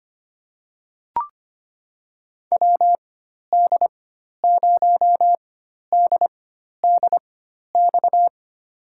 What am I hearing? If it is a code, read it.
Morse code, heard as a keyed tone, about 25 wpm, spaced as for 12 wpm, 700 Hz. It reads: WD0DDX
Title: USA Call Signs - 6 Characters - 4x Character Spacing - 25wpm